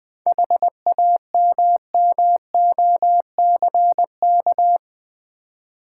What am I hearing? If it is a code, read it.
Morse code, heard as a keyed tone, about 20 wpm, 700 Hz. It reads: HAMMOCK
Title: Easy Words with Hints - 20wpm